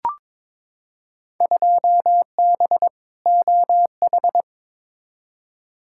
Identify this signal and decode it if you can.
Morse code, heard as a keyed tone, about 21 wpm, 700 Hz. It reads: 2B OH
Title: Field Day Exchanges Encore - 22wpm